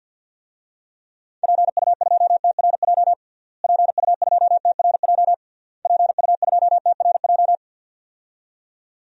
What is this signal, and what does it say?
Morse code, heard as a keyed tone, about 48 wpm, 700 Hz. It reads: PU2TRJ PU2TRJ PU2TRJ
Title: DXCC-100 Call Signs - 6 Characters - Repeated 3x - 50wpm